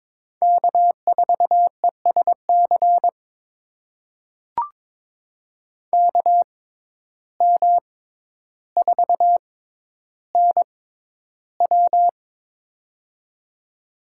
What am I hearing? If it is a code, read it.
Morse code, heard as a keyed tone, about 22 wpm, spaced as for 11 wpm, 700 Hz. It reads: K4ESC KM4NW
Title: USA Call Signs - 5 Characters - 6x Character Spacing - 22wpm